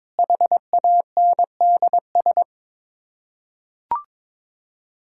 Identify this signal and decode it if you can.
Morse code, heard as a keyed tone, about 22 wpm, 700 Hz. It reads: HANDS
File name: practice / 22wpm / Sets of 8 Words - 22wpm